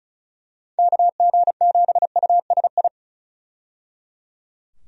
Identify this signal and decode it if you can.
Morse code, heard as a keyed tone, about 36 wpm, 700 Hz. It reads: KG7USI
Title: USA Call Signs - 6 Characters - 35wpm